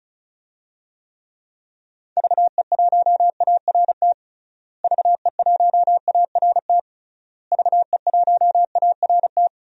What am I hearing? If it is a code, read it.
Morse code, heard as a keyed tone, about 36 wpm, 700 Hz. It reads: VE1ART VE1ART VE1ART
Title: DXCC-100 Call Signs - 6 Characters - Repeated 3x - 35wpm